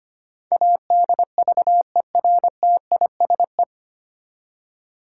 Morse code, heard as a keyed tone, about 25 wpm, 700 Hz. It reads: ADVERTISE
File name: practice / 25wpm / 9 Letter Words - First 3 Letters Given - 25wpm